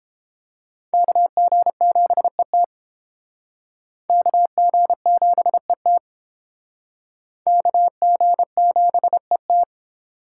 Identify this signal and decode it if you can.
Morse code, heard as a keyed tone, about 29 wpm, 700 Hz. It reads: KG7ET KG7ET KG7ET